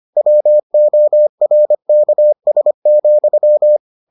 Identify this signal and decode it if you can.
Morse code, heard as a keyed tone, about 25 wpm, 600 Hz. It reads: WORKS,